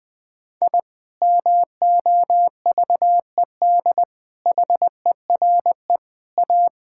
Morse code, heard as a keyed tone, about 20 wpm, 700 Hz. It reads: I MOVED HERE A
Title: Sentences with 6-Word Phrases - Spoken Beginning - 20wpm